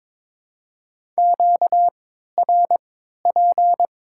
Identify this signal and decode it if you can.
Morse code, heard as a keyed tone, about 22 wpm, spaced as for 12 wpm, 700 Hz. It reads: QRP
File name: Top 90 QSO Elements - 3x Character Spacing - 22wpm